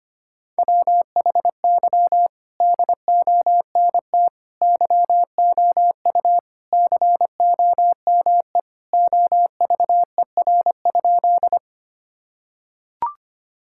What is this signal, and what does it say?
Morse code, heard as a keyed tone, about 25 wpm, 700 Hz. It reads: WHY DONT YOU COME OVER?
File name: practice / 25wpm / Sentences with 4-Word Phrases - 25wpm